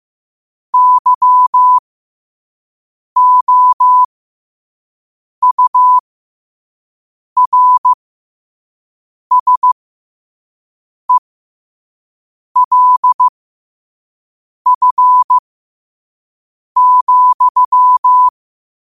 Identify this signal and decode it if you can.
Morse code, heard as a keyed tone, about 15 wpm, spaced as for 5 wpm, 1000 Hz. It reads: YOURSELF,
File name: qotdfiles / 5wpm